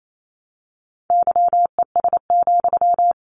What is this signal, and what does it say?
Morse code, heard as a keyed tone, about 28 wpm, 700 Hz. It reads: YES,
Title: Sentences from Top 700 Words 28wpm